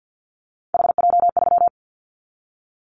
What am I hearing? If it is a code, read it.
Morse code, heard as a keyed tone, about 50 wpm, 700 Hz. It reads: HJ3
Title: Callsign Prefixes for Top 100 DXCC - 50wpm